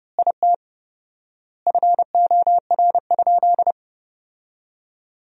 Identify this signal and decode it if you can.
Morse code, heard as a keyed tone, about 30 wpm, 700 Hz. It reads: IT FOR?
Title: 4x Word-Spacing - Sentences from Top 100 Words - 30wpm